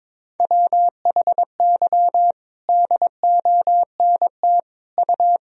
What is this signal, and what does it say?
Morse code, heard as a keyed tone, about 21 wpm, 700 Hz. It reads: WHY DONT U
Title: Sentences from Top 300 Words and Abbreviations 22wpm